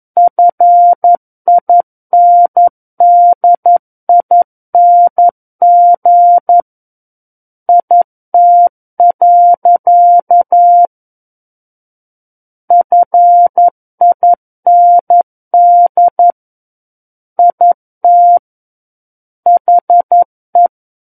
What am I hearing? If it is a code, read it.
Morse code, heard as a keyed tone, about 11 wpm, 700 Hz. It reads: FINDING IT. FIND IT HE
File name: 11 WPM